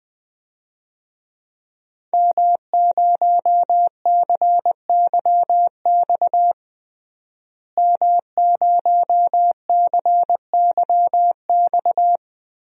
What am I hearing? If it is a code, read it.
Morse code, heard as a keyed tone, about 20 wpm, 700 Hz. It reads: M0CYX M0CYX
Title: DXCC-100 Call Signs - 5 Characters - Repeated 3x - 20wpm